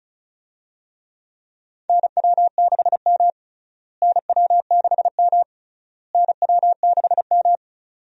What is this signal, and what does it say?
Morse code, heard as a keyed tone, about 36 wpm, 700 Hz. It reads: NW6M NW6M NW6M
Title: USA Call Signs - 4 Characters - Repeated 3x - 35wpm